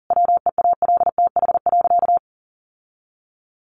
Morse code, heard as a keyed tone, about 40 wpm, 700 Hz. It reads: WEALTH.
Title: Sentences from Top 1200 Words 40wpm